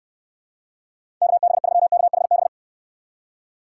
Morse code, heard as a keyed tone, about 68 wpm, 700 Hz. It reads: KB3KHB